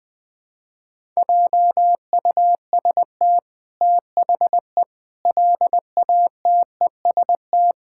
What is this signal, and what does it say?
Morse code, heard as a keyed tone, about 21 wpm, 700 Hz. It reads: JUST THE LATEST
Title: Sets of 8 Words - 20wpm